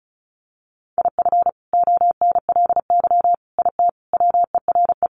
Morse code, heard as a keyed tone, about 34 wpm, 700 Hz. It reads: IF ONLY IT WERE
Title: Sentences from Top 500 Words 35wpm